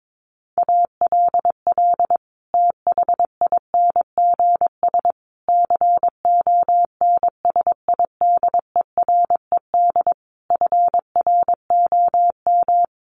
Morse code, heard as a keyed tone, about 22 wpm, 700 Hz. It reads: ALL THINGS CONSIDERED FROM
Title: Sets of 5 Words Encore 22wpm